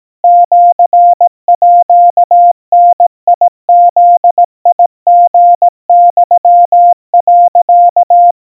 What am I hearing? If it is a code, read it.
Morse code (Japanese Wabun code), heard as a keyed tone, about 18 wpm, 700 Hz. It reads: シーダブリユ、
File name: 60 cpm 700 Hz Crispy